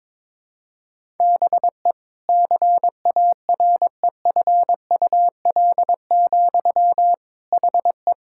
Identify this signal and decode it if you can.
Morse code, heard as a keyed tone, about 22 wpm, 700 Hz. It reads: BE CAREFUL, HE
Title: Sentences from Top 1600 Words 22wpm